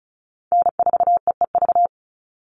Morse code, heard as a keyed tone, about 34 wpm, 700 Hz. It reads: N4EEV
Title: Call Signs - 35wpm